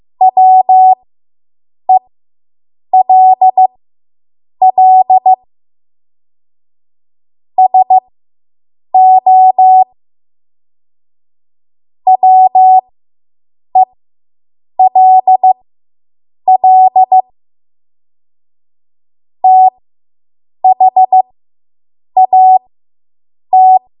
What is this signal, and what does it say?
Morse code, heard as a keyed tone, about 15 wpm, spaced as for 7 wpm, 750 Hz. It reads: WELL SO WELL THAT